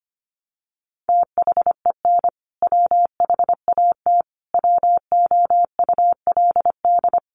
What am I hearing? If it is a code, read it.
Morse code, heard as a keyed tone, about 25 wpm, 700 Hz. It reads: THEN WHAT WOULD
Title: Sentences from Top 400 Words 25wpm